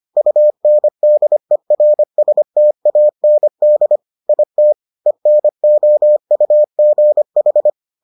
Morse code, heard as a keyed tone, about 25 wpm, 600 Hz. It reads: UNDERSTAND IT ENOUGH